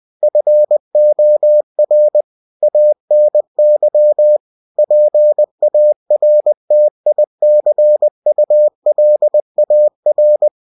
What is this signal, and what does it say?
Morse code, heard as a keyed tone, about 20 wpm, 600 Hz. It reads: FOR ANY PARTICULAR
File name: cw-books / winnie-the-pooh / winnie-20wpm-page-0121